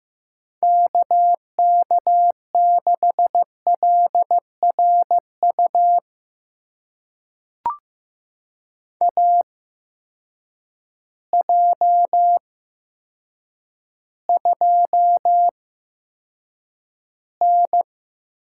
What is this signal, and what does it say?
Morse code, heard as a keyed tone, about 15 wpm, spaced as for 8 wpm, 700 Hz. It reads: KK6LRU AJ2N
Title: USA Call Signs - 8x Character Spacing - 15wpm